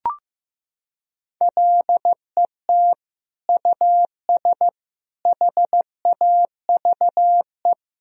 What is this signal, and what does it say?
Morse code, heard as a keyed tone, about 15 wpm, 700 Hz. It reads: LET US HAVE
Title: Sentences from Top 1000 Words - Review Part 2 - 15wpm